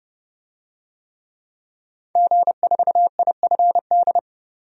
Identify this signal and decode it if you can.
Morse code, heard as a keyed tone, about 32 wpm, 700 Hz. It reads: G4IFD
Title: DXCC-100 Call Signs - 5 Characters - 30wpm